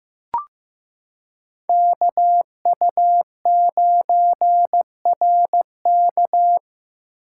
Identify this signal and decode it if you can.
Morse code, heard as a keyed tone, about 15 wpm, 700 Hz. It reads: KU9RK
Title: K1USN SST QSO Practice - 15wpm